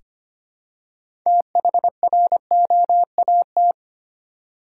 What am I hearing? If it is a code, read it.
Morse code, heard as a keyed tone, about 25 wpm, 700 Hz. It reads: THROAT
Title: Six Letter Words - 25wpm